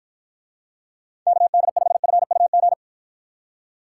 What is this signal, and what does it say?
Morse code, heard as a keyed tone, about 53 wpm, 700 Hz. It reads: KD5FUG